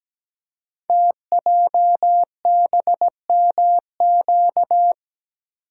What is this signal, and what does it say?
Morse code, heard as a keyed tone, about 17 wpm, 700 Hz. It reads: TJBMQ
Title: Code Groups - 5 Letters - 17wpm